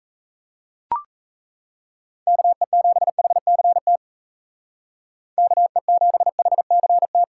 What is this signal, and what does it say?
Morse code, heard as a keyed tone, about 33 wpm, 700 Hz. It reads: KE7HCT KE7HCT
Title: Speed Racing - USA 2x3 Call Signs 28wpm